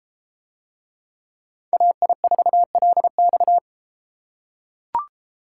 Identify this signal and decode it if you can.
Morse code, heard as a keyed tone, about 33 wpm, 700 Hz. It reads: AI4LX